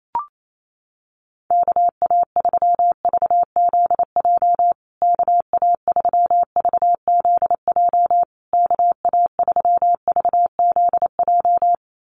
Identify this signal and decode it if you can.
Morse code, heard as a keyed tone, about 28 wpm, 690 Hz. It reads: KA3VZJ KA3VZJ KA3VZJ
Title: USA Call Signs Repeated 3x 28wpm